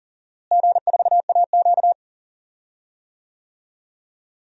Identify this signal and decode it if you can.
Morse code, heard as a keyed tone, about 40 wpm, 700 Hz. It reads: G4AQ
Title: DXCC-100 Call Signs - 4 Characters - Repeated 2x - 40wpm